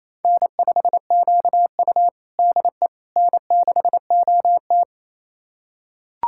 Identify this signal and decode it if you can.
Morse code, heard as a keyed tone, about 27 wpm, 690 Hz. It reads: N5QU DE N6OT
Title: USA Callsign DE Callsign - 4 Characters - 28wpm